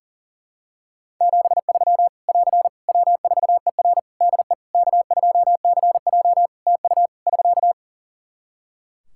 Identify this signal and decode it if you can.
Morse code, heard as a keyed tone, about 40 wpm, 700 Hz. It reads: 73 <AR> W4ER DE K2CJ TU <SK>
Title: Practice QSO - Exchange 4 - Ending - 40wpm